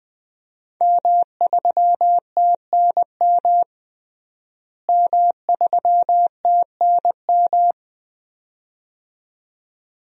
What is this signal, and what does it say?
Morse code, heard as a keyed tone, about 20 wpm, 700 Hz. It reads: M3TNM M3TNM
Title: DXCC-100 Call Signs - 5 Characters - Repeated 2x - 20wpm